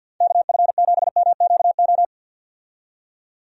Morse code, heard as a keyed tone, about 46 wpm, 700 Hz. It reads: KF7MQO
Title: USA Call Signs - 6 Characters - 50wpm